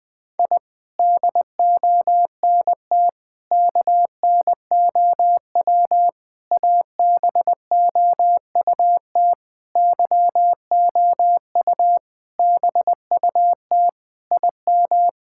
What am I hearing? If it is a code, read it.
Morse code, heard as a keyed tone, about 21 wpm, 700 Hz. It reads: I DONT KNOW ABOUT YOU BUT IM
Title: Sentences with 6-Word Phrases - 20wpm